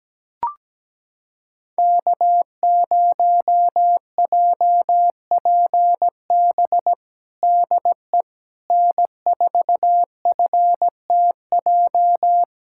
Morse code, heard as a keyed tone, about 17 wpm, 700 Hz. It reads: K0JPB DE N4FTJ
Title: USA Callsign DE Callsign - 5 Characters - 17wpm